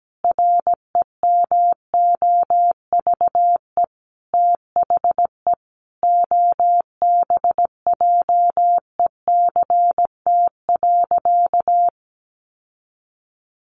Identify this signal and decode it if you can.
Morse code, heard as a keyed tone, about 17 wpm, 700 Hz. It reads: REMOVE THE OBJECT.
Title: Sentences from Top 800 Words 17wpm